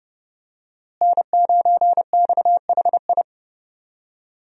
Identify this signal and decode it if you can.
Morse code, heard as a keyed tone, about 32 wpm, 700 Hz. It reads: N9XHI
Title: USA Call Signs - 5 Characters - Repeated 3x - 30wpm